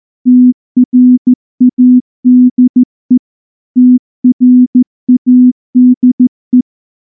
Morse code, heard as a keyed tone, about 15 wpm, 250 Hz. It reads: TRADE TRADE
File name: morse / morse-training-15WPM